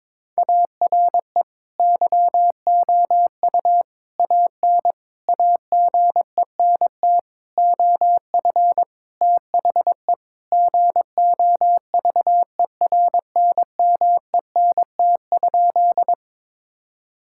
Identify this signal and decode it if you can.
Morse code, heard as a keyed tone, about 21 wpm, 700 Hz. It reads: ARE YOU AN AGENT OF THE GOVERNMENT?
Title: Sentences from Top 1600 Words 22wpm